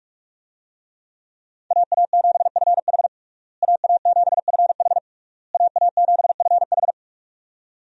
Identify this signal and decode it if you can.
Morse code, heard as a keyed tone, about 46 wpm, 700 Hz. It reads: AA7FH AA7FH AA7FH
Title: USA Call Signs - 5 Characters - Repeated 3x - 45wpm